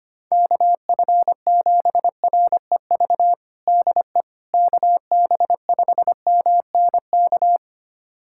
Morse code, heard as a keyed tone, about 25 wpm, 700 Hz. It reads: KF7REV DE KB5MNK
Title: USA Callsign DE Callsign - 6 Characters - 25wpm